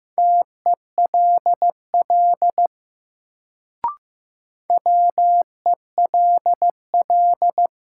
Morse code, heard as a keyed tone, about 15 wpm, 700 Hz. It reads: TELL WELL
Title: Sets of 2 Words - New Word Drill - 15wpm